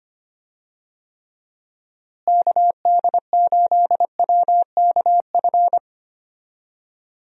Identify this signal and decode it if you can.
Morse code, heard as a keyed tone, about 25 wpm, 700 Hz. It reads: KD8WKF
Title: USA Call Signs - 6 Characters - Repeated 3x - 25wpm